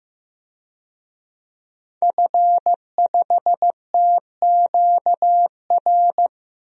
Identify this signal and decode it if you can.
Morse code, heard as a keyed tone, about 15 wpm, 700 Hz. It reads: F5TQR
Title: DXCC-100 Call Signs - 5 Characters - Repeated 3x - 15wpm